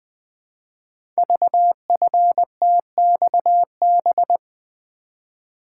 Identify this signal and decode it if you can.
Morse code, heard as a keyed tone, about 20 wpm, 700 Hz. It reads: VFTXB
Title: Code Groups - 5 Letters - 20wpm